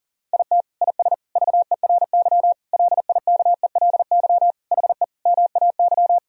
Morse code, heard as a keyed tone, about 40 wpm, 700 Hz. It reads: IT IS VERY LIKELY HE MAY
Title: Sentences from Top 1000 Words - Review Part 4 - 40wpm